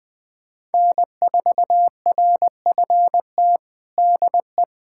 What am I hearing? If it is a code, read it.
Morse code, heard as a keyed tone, about 20 wpm, 700 Hz. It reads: N4RFT DE
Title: USA Callsign DE Callsign 20wpm